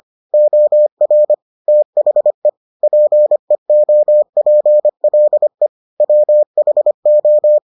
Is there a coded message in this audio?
Yes: Morse code from 0.3 to 7.6 s, heard as a keyed tone, about 25 wpm, 600 Hz. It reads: OR THE PEOPLE WHO